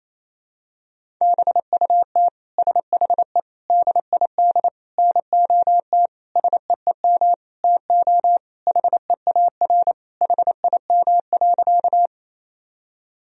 Morse code, heard as a keyed tone, about 28 wpm, 700 Hz. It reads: BUT SHE DID NOT SEEM TO HEAR HIM.
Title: Sentences from Top 1000 Words - Review Part 1 - 28wpm